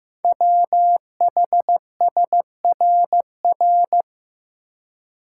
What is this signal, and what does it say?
Morse code, heard as a keyed tone, about 15 wpm, 700 Hz. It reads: WHSRR
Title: Code Groups - 5 Letters - 15wpm